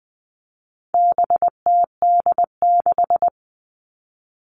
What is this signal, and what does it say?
Morse code, heard as a keyed tone, about 20 wpm, 700 Hz. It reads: BTD6
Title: All Cut Numbers - 4-Digit 20wpm